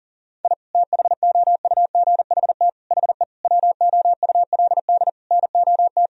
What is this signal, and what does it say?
Morse code, heard as a keyed tone, about 40 wpm, 700 Hz. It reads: I THOUGHT HE WOULD NOT
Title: Sentences from Top 1000 Words - Review Part 1 - 40wpm